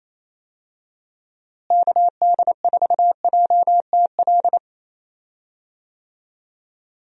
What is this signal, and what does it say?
Morse code, heard as a keyed tone, about 27 wpm, 700 Hz. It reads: KD4JTL
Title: USA Call Signs - 6 Characters - Repeated 2x - 28wpm